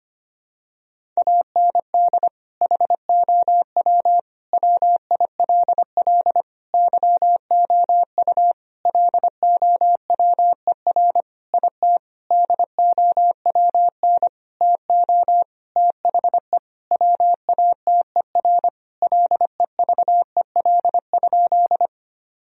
Morse code, heard as a keyed tone, about 25 wpm, 700 Hz. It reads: AND HOW WILL YOU LOWER IT DOWN TO THE WATER LEVEL?
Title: Sentences from Top 1000 Words - Review Part 4 - 25wpm